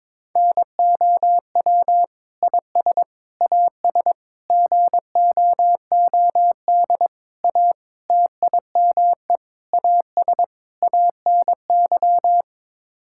Morse code, heard as a keyed tone, about 22 wpm, 700 Hz. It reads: NOW IS AS GOOD A TIME AS ANY
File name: practice / 22wpm / Sets of 8 Words - 22wpm